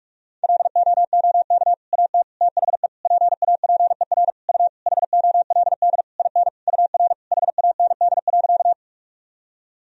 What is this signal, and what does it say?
Morse code, heard as a keyed tone, about 44 wpm, 700 Hz. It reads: LOOK AT THE PAPER U HOLD IN UR HAND.
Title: Sentences from Top 400 Words and Abbreviations 45wpm